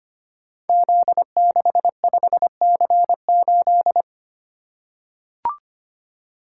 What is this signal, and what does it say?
Morse code, heard as a keyed tone, about 25 wpm, 700 Hz. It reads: Z65C8
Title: Code Groups - 5 Letter:Numbers - 25wpm